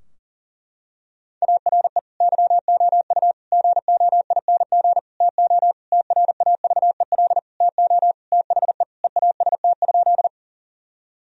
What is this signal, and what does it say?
Morse code, heard as a keyed tone, about 40 wpm, 700 Hz. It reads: ARE YOU GOING TO TRAVEL TO THE EAST?
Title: Sentences from Top 1000 Words - Review Part 3 - 40wpm